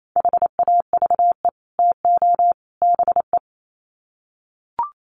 Morse code, heard as a keyed tone, about 28 wpm, 700 Hz. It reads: HAVE TO BE
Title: Sets of 3 Words 28wpm